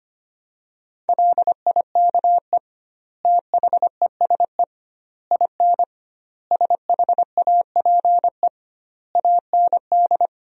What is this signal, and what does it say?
Morse code, heard as a keyed tone, about 25 wpm, 700 Hz. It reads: LIKE THESE IN SHAPE AND
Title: 2x Word-Spacing - Sentences from Top 500 Words - 25wpm